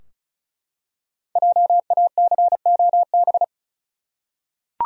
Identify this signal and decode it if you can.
Morse code, heard as a keyed tone, about 34 wpm, 700 Hz. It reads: JACOB
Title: USA Common Names - Max 5 Letters - 35wpm